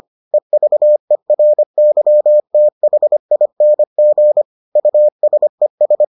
Morse code, heard as a keyed tone, about 25 wpm, 600 Hz. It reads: EVERYTHING USES